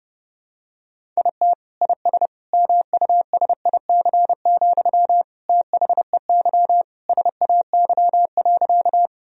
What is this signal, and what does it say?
Morse code, heard as a keyed tone, about 30 wpm, 700 Hz. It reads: IT IS MUSIC, THEY SAY.
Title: Sentences from Top 1000 Words - Review Part 1 - 30wpm